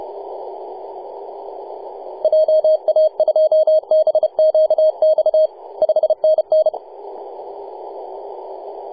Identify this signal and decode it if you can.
Morse code, heard as a keyed tone, about 31 wpm, 620 Hz. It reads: JA2BQX 5NN
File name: posts / 2015-nov-18-0552z-vk9wa-24-894mhz-cw